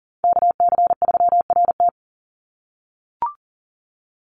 Morse code, heard as a keyed tone, about 40 wpm, 700 Hz. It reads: KC3RT
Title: NA SOTA Chaser Callsigns 40wpm